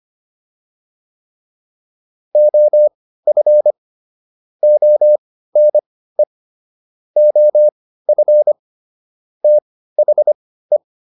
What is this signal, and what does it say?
Morse code, heard as a keyed tone, about 25 wpm, spaced as for 15 wpm, 600 Hz. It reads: OF ONE OF THE